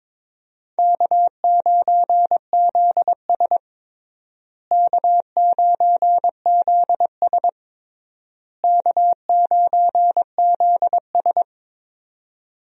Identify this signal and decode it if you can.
Morse code, heard as a keyed tone, about 22 wpm, 700 Hz. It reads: K9ZS K9ZS K9ZS
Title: USA Call Signs - 4 Characters - Repeated 3x - 22wpm